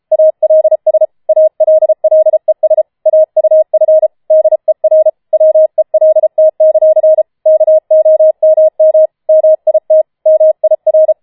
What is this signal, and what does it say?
Morse code, heard as a keyed tone, about 33 wpm, 630 Hz. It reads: ALS ALLES AUF DER WELT; KOMM MIT MIR
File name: Schneewittchen Teil5 34wpm